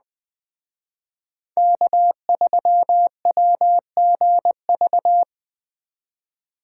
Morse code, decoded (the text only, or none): K3WGV